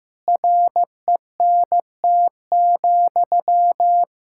Morse code, heard as a keyed tone, about 15 wpm, 700 Hz. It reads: RENT,
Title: Sentences from Top 1000 Words - Review Part 5 - 15wpm